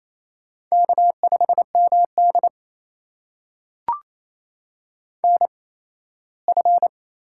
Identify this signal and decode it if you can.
Morse code, heard as a keyed tone, about 28 wpm, spaced as for 17 wpm, 700 Hz. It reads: K5MD NF
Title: USA Call Signs - 4 Characters - 8x Character Spacing - 28wpm